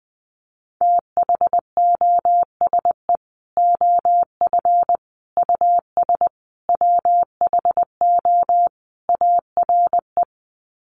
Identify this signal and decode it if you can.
Morse code, heard as a keyed tone, about 20 wpm, 700 Hz. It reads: THOSE OF US WHO ARE
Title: Sets of 5 Words Encore2 20wpm